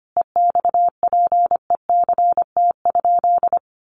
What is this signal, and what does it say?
Morse code, heard as a keyed tone, about 25 wpm, 700 Hz. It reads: EXPECT?
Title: Sentences from Top 1100 Words 25wpm